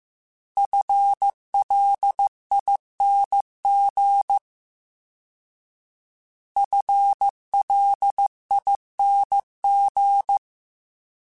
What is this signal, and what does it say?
Morse code, heard as a keyed tone, about 15 wpm, 800 Hz. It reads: FLING FLING
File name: FL / FL mp3s / FL Prefixes 3x2 Twice @13